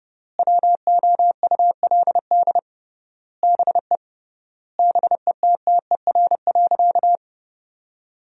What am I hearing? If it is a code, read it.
Morse code, heard as a keyed tone, about 32 wpm, 700 Hz. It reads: WOULD BE BETTER.
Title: 3x Word-Spacing - Sentences from Top 400 Words - 30wpm